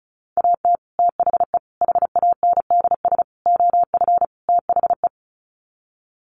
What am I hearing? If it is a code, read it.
Morse code, heard as a keyed tone, about 34 wpm, 700 Hz. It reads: AT THE HANDS OF THE
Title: Sets of 5 Words Encore2 35wpm